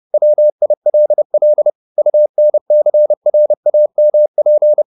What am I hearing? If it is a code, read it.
Morse code, heard as a keyed tone, about 30 wpm, 600 Hz. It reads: WILL UNCRAMP
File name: the old man and the sea / the old man and the sea 1 30wpm